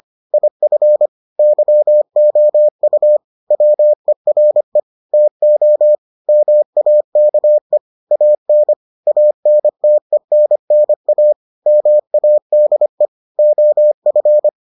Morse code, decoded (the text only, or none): IF YOU WERE TO MAKE AN ANTENNA MADE OF